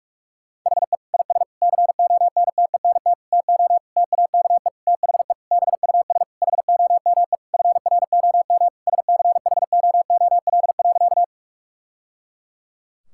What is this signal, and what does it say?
Morse code, heard as a keyed tone, about 46 wpm, 700 Hz. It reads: HE IS CONTENT TO TAKE THE BUS HOME FROM SCHOOL.
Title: Sentences from Top 1000 Words - Review Part 6 - 45wpm